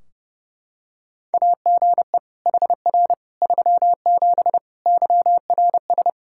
Morse code, heard as a keyed tone, about 30 wpm, 700 Hz. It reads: AGE HR 37 YRS
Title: Practice QSO - Exchange 3 - More Info - 30wpm